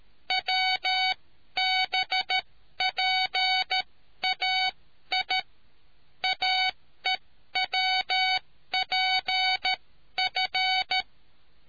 Morse code, heard as a keyed tone, about 13 wpm, spaced as for 11 wpm, 750 Hz. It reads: WBPAI AEWPF